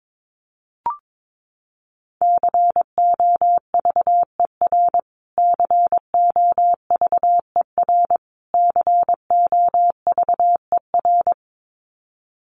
Morse code, heard as a keyed tone, about 22 wpm, 700 Hz. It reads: COVER COVER COVER
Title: Top 500 Words Repeated 22wpm